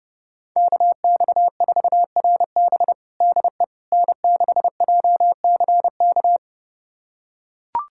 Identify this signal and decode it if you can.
Morse code, heard as a keyed tone, about 30 wpm, 700 Hz. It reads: KX4RB DE N6JCK
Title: USA Callsign DE Callsign - 5 Characters - 30wpm